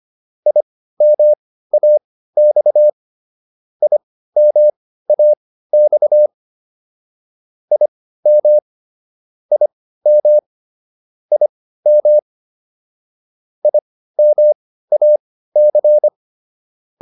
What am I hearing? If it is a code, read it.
Morse code, heard as a keyed tone, about 25 wpm, spaced as for 15 wpm, 600 Hz. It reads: IMAX IMAX IM IM IM IMAC
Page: intermediate-practice-files